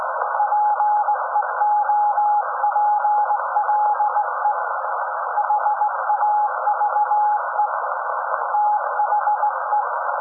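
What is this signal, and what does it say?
Morse code, heard as a keyed tone, about 18 wpm, 840 Hz. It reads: PEOPLE HAVE THE
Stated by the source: SAQ Grimeton Radio, Alexanderson 200 kW alternator on 17.2 kHz, received in Potsdam with a very strong signal on World Radio Day.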